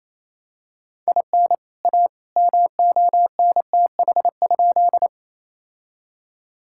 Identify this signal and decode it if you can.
Morse code, heard as a keyed tone, about 28 wpm, 700 Hz. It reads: IN A MONTH?
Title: Sentences from Top 1000 Words - Review Part 5 - 28wpm